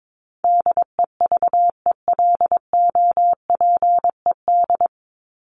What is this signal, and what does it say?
Morse code, heard as a keyed tone, about 22 wpm, 700 Hz. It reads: DEVELOPED
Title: ED Suffix - Top 100 Words with Base Words 22wpm